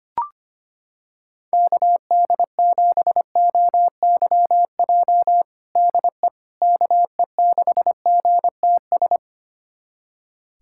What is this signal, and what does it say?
Morse code, heard as a keyed tone, about 25 wpm, 700 Hz. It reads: KD7OYJ DE KE6GTS
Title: USA Callsign DE Callsign - 6 Characters - 25wpm